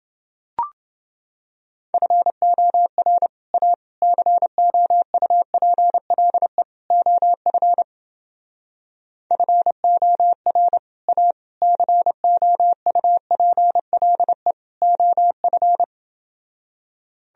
Morse code, heard as a keyed tone, about 29 wpm, 700 Hz. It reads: FOR A COUPLE OF FOR A COUPLE OF